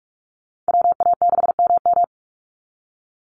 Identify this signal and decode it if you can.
Morse code, heard as a keyed tone, about 46 wpm, 700 Hz. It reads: WA6MM
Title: NA SOTA Activator Callsigns 45wpm